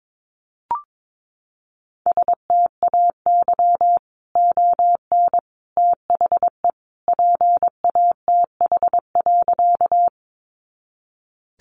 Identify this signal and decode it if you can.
Morse code, heard as a keyed tone, about 22 wpm, 700 Hz. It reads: STAY ON THE PATH.
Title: A Sound Of Thunder 22wpm